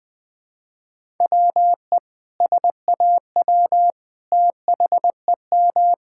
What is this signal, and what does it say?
Morse code, heard as a keyed tone, about 20 wpm, 700 Hz. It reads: WE SAW THEM
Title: Sentences with 4-Word Phrases - Spoken Beginning - 20wpm